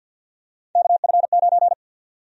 Morse code, heard as a keyed tone, about 51 wpm, 700 Hz. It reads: KF9
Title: USA Call Sign Prefixes - 50wpm